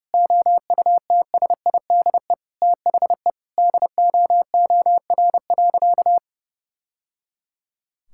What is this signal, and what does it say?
Morse code, heard as a keyed tone, about 30 wpm, 700 Hz. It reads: OUTSIDE THE DOOR.
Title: Sentences from Top 1000 Words - Review Part 6 - 30wpm